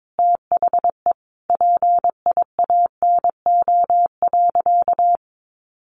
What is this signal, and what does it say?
Morse code, heard as a keyed tone, about 22 wpm, 700 Hz. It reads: THE PIANO.
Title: Additional Easy Phrases 22wpm